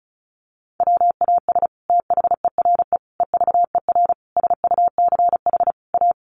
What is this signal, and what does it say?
Morse code, heard as a keyed tone, about 34 wpm, 700 Hz. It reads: WAS THERE EVER SUCH A
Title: Sentences from Top 400 Words 35wpm